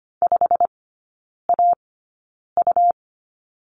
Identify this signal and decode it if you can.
Morse code, heard as a keyed tone, about 25 wpm, spaced as for 9 wpm, 700 Hz. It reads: 5AU
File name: Lesson 17e - Letter u - Three Characters 25wpm